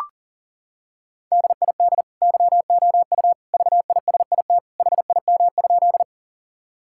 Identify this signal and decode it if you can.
Morse code, heard as a keyed tone, about 40 wpm, 700 Hz. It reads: DID YOU VISIT HIM?
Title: Sentences from Top 1000 Words - Review Part 6 - 40wpm